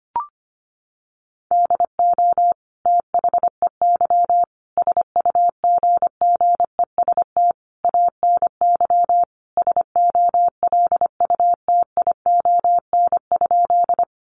Morse code, heard as a keyed tone, about 25 wpm, 700 Hz. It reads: DO THEY SUGGEST ANY SOLUTION?
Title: Sentences from Top 1300 Words 25wpm